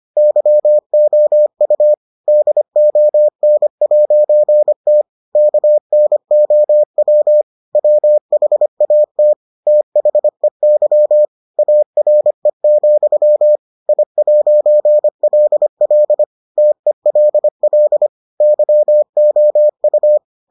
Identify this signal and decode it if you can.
Morse code, heard as a keyed tone, about 25 wpm, 600 Hz. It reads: YOU DON'T KNOW WHAT THEY ARE, I'LL TELL YOU